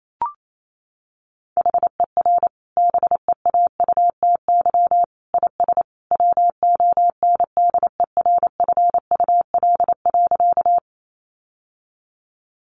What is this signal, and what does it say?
Morse code, heard as a keyed tone, about 28 wpm, 700 Hz. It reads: HER BEAUTY IS WONDERFUL.
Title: Sentences from Top 1100 Words 28wpm